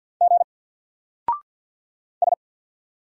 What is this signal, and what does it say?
Morse code, heard as a keyed tone, about 48 wpm, 700 Hz. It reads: G S
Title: Single Letter-Number Alt 50wpm